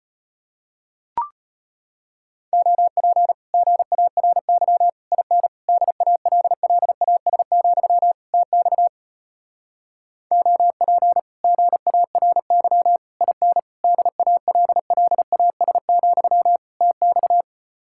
Morse code, heard as a keyed tone, about 33 wpm, 700 Hz. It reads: OP GARY IN DALLAS, TX OP GARY IN DALLAS, TX